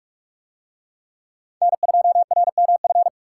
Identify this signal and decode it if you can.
Morse code, heard as a keyed tone, about 42 wpm, 700 Hz. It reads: N2RMF